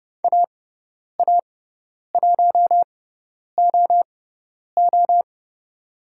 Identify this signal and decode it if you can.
Morse code, heard as a keyed tone, about 30 wpm, spaced as for 10 wpm, 700 Hz. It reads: AA1OO
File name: Lesson 16g - Letter c - Call Signs 30wpm